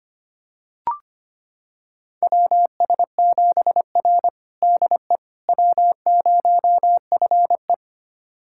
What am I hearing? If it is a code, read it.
Morse code, heard as a keyed tone, about 25 wpm, 700 Hz. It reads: WS7R DE W0FE
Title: USA Callsign DE Callsign - 4 Characters - 25wpm